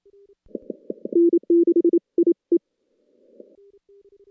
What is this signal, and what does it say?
Morse code, heard as a keyed tone, about 27 wpm, 340 Hz. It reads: N6IE
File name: contests / CW-SS-k5tr- -2300